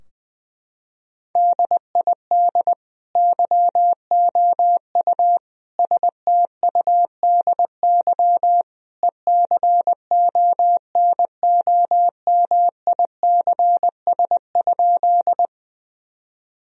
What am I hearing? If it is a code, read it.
Morse code, heard as a keyed tone, about 20 wpm, 700 Hz. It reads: DID YOU STUDY ECONOMICS?
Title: Sentences from Top 1000 Words - Review Part 3 - 20wpm